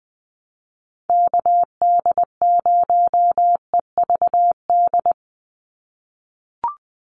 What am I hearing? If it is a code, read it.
Morse code, heard as a keyed tone, about 20 wpm, 700 Hz. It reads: KD0EVD
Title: USA Call Signs 20wpm